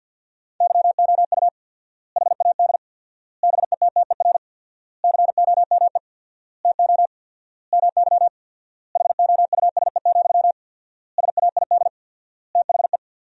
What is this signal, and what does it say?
Morse code, heard as a keyed tone, about 50 wpm, 700 Hz. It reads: YOU HAD BETTER COME TO MY HOUSE, SAID THE
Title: 4x Word-Spacing - Sentences from Top 400 Words - 50wpm